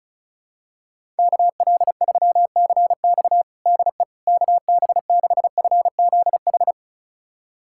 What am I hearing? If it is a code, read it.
Morse code, heard as a keyed tone, about 34 wpm, 700 Hz. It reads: KL3CX DE KB6FZH
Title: USA Callsign DE Callsign 35wpm